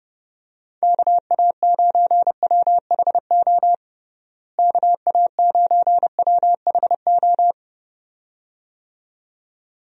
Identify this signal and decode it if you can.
Morse code, heard as a keyed tone, about 30 wpm, 700 Hz. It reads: KA9WHO KA9WHO
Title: USA Call Signs - 6 Characters - Repeated 2x - 30wpm